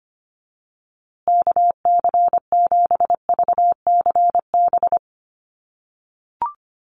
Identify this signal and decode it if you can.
Morse code, heard as a keyed tone, about 25 wpm, 700 Hz. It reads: KC7VCB
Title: USA Call Signs Repeated 2x 25wpm